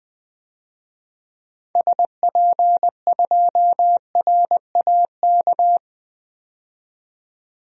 Morse code, heard as a keyed tone, about 20 wpm, 700 Hz. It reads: SP2RAK
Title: DXCC-100 Call Signs - 6 Characters - 20wpm